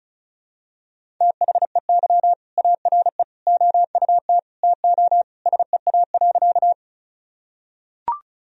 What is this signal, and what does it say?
Morse code, heard as a keyed tone, about 36 wpm, 700 Hz. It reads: THEY ARE OUT TO SEA.
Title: Sentences from Top 1000 Words - Review Part 1 - 35wpm